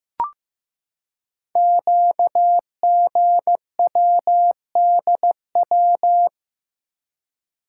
Morse code, heard as a keyed tone, about 15 wpm, 700 Hz. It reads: QGWDW
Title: Code Groups - 5 Letters - 15wpm